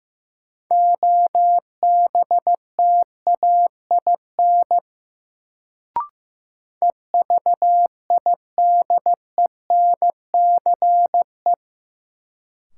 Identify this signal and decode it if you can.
Morse code, heard as a keyed tone, about 15 wpm, 700 Hz. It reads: OBTAIN EVIDENCE
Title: Sentences from Top 1300 Words - New Word Drill - 15wpm